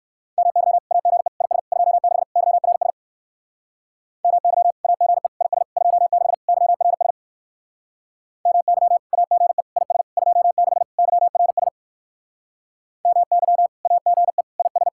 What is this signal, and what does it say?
Morse code, heard as a keyed tone, about 67 wpm, 700 Hz. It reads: MY AGE IS 26 YRS MY AGE IS 26 YRS MY AGE IS 26 YRS MY AGE IS